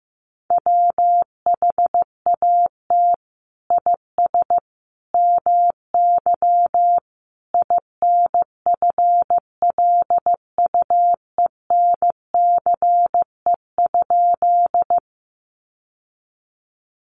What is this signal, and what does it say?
Morse code, heard as a keyed tone, about 15 wpm, 700 Hz. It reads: WHAT IS MY INFLUENCE?
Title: Sentences from Top 1100 Words 15wpm